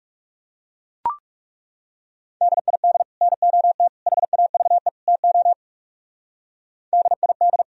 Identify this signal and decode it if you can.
Morse code, heard as a keyed tone, about 37 wpm, 700 Hz. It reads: DID NOT HAVE TO DID
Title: Speed-Racing - Sets of 4 Words 30wpm